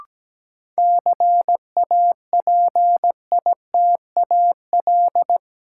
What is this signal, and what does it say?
Morse code, heard as a keyed tone, about 17 wpm, 700 Hz. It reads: CAPITAL
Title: Sentences from Top 1000 Words - New Word Drill - 17wpm